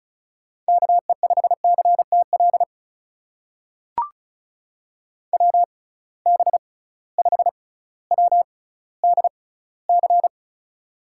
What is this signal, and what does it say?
Morse code, heard as a keyed tone, about 36 wpm, spaced as for 18 wpm, 700 Hz. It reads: KE5CTL WB5WDC
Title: USA Call Signs - 6 Characters - 6x Character Spacing - 35wpm